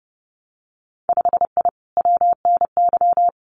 Morse code, heard as a keyed tone, about 30 wpm, 700 Hz. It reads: 5I WNY
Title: Winter Field Day Exchanges Encore 30wpm